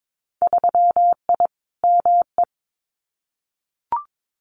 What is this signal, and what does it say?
Morse code, heard as a keyed tone, about 22 wpm, 700 Hz. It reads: 3I ME
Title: Winter Field Day Exchanges 22wpm